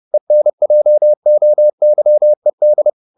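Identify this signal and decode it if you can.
Morse code, heard as a keyed tone, about 30 wpm, 600 Hz. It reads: ENJOYED
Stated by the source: Morse code at 30 wpm, the novel Never by Ken Follett in English, from CW med Gnister.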